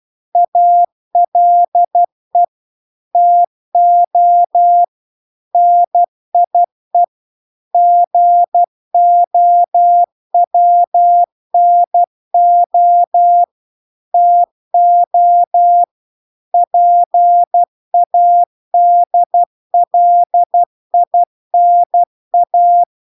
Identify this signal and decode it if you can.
Morse code, heard as a keyed tone, about 12 wpm, 700 Hz. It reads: ALE TO NIE GOWNO TO PADLINA